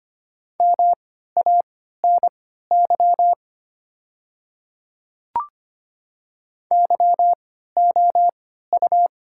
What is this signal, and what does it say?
Morse code, heard as a keyed tone, about 25 wpm, spaced as for 14 wpm, 700 Hz. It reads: MANY YOU
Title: 3x Character Spacing - Top 100 Words - 25wpm